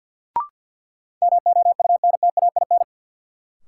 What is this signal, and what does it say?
Morse code, heard as a keyed tone, about 53 wpm, 700 Hz. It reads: MOUNTAIN
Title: Sentences from Top 1000 Words - Review Part 1 - New Word Drill - 50wpm